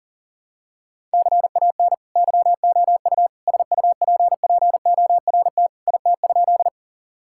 Morse code, heard as a keyed tone, about 40 wpm, 700 Hz. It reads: CAN YOU SUPPORT IT?
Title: Sentences from Top 1000 Words - Review Part 5 - 40wpm